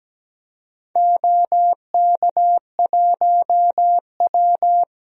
Morse code, heard as a keyed tone, about 17 wpm, 700 Hz. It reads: OK1W
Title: DXCC-100 Call Signs - 4 Characters - 17wpm